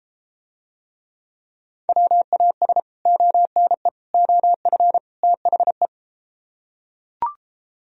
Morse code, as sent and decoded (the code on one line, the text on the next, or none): .-- .- ... / --- -. / --- ..-. / - .... .
WAS ONE OF THE